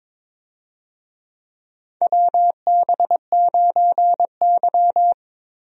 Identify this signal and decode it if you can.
Morse code, heard as a keyed tone, about 22 wpm, 700 Hz. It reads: WB9Y
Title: USA Call Signs - 4 Characters - Repeated 3x - 22wpm